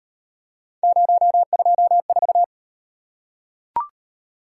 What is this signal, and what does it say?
Morse code, heard as a keyed tone, about 38 wpm, 700 Hz. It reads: 024